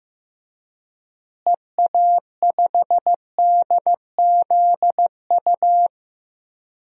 Morse code, heard as a keyed tone, about 15 wpm, 700 Hz. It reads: EA5DZU
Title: DXCC-100 Call Signs - 6 Characters - Repeated 3x - 15wpm